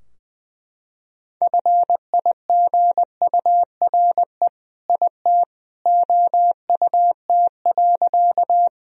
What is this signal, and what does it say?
Morse code, heard as a keyed tone, about 20 wpm, 700 Hz. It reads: FIGURE IT OUT.
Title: Sentences from Top 1000 Words - Review Part 2 - 20wpm